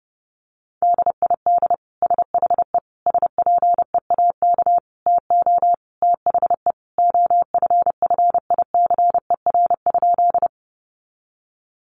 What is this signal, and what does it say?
Morse code, heard as a keyed tone, about 30 wpm, 700 Hz. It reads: DID SHE SPEAK TO THE OFFICER?
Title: Sentences from Top 1200 Words 30wpm